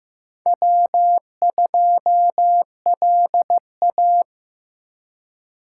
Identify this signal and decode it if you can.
Morse code, heard as a keyed tone, about 18 wpm, 700 Hz. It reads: W2LA
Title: Speed Racing - USA 1x2 Call Signs 15wpm